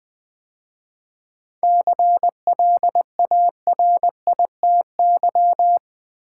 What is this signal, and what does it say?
Morse code, heard as a keyed tone, about 20 wpm, 700 Hz. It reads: CLARITY
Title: 7 Letter Words - First 3 Letters Given - 20wpm